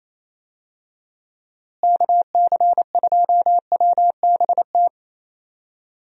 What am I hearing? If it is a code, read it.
Morse code, heard as a keyed tone, about 28 wpm, 700 Hz. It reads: KC2WBT
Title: USA Call Signs - 6 Characters - Repeated 3x - 28wpm